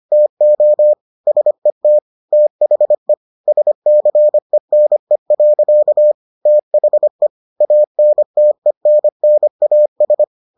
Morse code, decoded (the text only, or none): TO SET THE SCENE. THE ANTENNAS